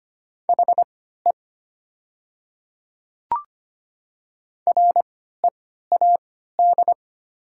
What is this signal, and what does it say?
Morse code, heard as a keyed tone, about 25 wpm, spaced as for 14 wpm, 700 Hz. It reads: HE READ
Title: 3x Character Spacing - Top 300 Words - 25wpm